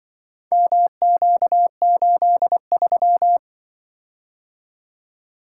Morse code, heard as a keyed tone, about 24 wpm, 700 Hz. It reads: MQ83
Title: Speed-Racing - Four Letter-Number Combinations - 20wpm